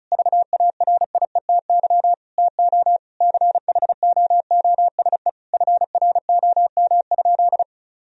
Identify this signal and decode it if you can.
Morse code, heard as a keyed tone, about 36 wpm, 700 Hz. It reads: VARIETY TO CHOOSE FROM?
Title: Sentences from Top 1000 Words - Review Part 4 - 35wpm